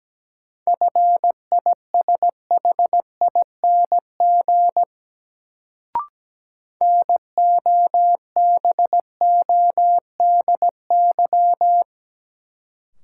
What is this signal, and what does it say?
Morse code, heard as a keyed tone, about 17 wpm, 700 Hz. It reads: FISHING NOBODY
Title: Sentences from Top 1600 Words - New Word Drill Alt - 17wpm